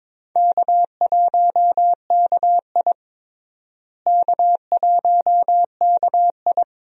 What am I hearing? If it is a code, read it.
Morse code, heard as a keyed tone, about 22 wpm, 700 Hz. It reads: K1KI K1KI
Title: USA Call Signs - 4 Characters - Repeated 2x - 22wpm